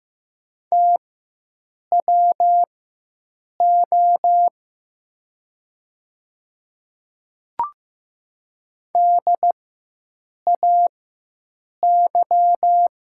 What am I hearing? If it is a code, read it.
Morse code, heard as a keyed tone, about 15 wpm, spaced as for 7 wpm, 700 Hz. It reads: TWO DAY
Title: 4x Character Spacing - Top 100 Words - 15wpm